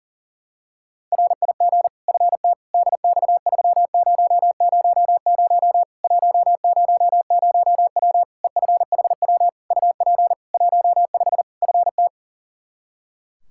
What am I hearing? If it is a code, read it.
Morse code, heard as a keyed tone, about 40 wpm, 700 Hz. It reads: RIG FT DX3000 100W EFHW UP 15 FT
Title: Practice QSO - Exchange 2 - Info - 40wpm